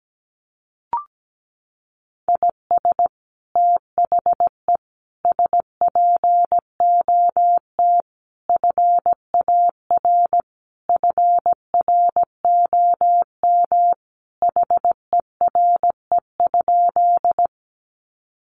Morse code, heard as a keyed tone, about 17 wpm, 700 Hz. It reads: IS THE SPOT FAR FROM HERE?
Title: Sentences from Top 1100 Words 17wpm